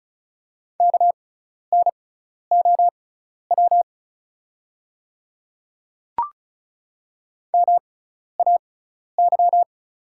Morse code, heard as a keyed tone, about 34 wpm, spaced as for 12 wpm, 700 Hz. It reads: KNOW MAY
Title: 6x Character Spacing - Top 100 Words - 35wpm